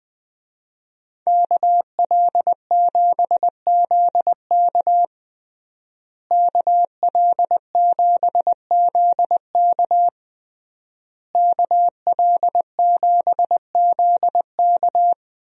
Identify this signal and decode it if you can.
Morse code, heard as a keyed tone, about 21 wpm, 700 Hz. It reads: KL7ZK KL7ZK KL7ZK
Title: USA Call Signs - 5 Characters - Repeated 3x - 20wpm